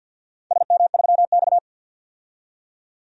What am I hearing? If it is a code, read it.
Morse code, heard as a keyed tone, about 51 wpm, 700 Hz. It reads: SM3X
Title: DXCC-100 Call Signs - 4 Characters - 50wpm